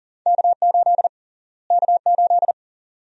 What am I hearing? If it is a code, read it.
Morse code, heard as a keyed tone, about 40 wpm, 700 Hz. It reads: K8 K8
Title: USA Call Sign Prefixes - Repeated 2x - 40wpm